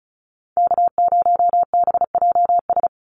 Morse code, heard as a keyed tone, about 34 wpm, 700 Hz. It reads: K0BJS
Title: USA Call Signs 35wpm